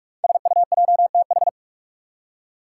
Morse code, heard as a keyed tone, about 46 wpm, 700 Hz. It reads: SUJTH
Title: Code Groups - 5 Letter:Numbers - 45wpm